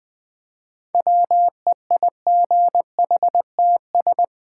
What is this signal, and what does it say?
Morse code, heard as a keyed tone, about 20 wpm, 700 Hz. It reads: WEIGHTS
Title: Easy Words with Hints - 20wpm